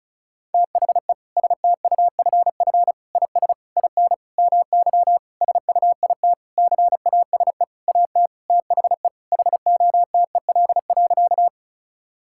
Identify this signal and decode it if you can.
Morse code, heard as a keyed tone, about 36 wpm, 700 Hz. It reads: THE STUFF IS IN MY SUITCASE AT THE HOTEL.
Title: Sentences from Top 1600 Words 35wpm